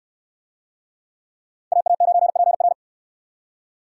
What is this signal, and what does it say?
Morse code, heard as a keyed tone, about 62 wpm, 700 Hz. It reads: AA9PR